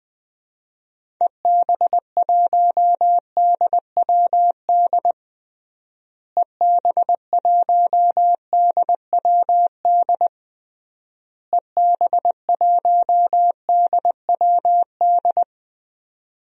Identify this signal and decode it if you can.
Morse code, heard as a keyed tone, about 20 wpm, 700 Hz. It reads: EB1DWD EB1DWD EB1DWD
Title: DXCC-100 Call Signs - 6 Characters - Repeated 3x - 20wpm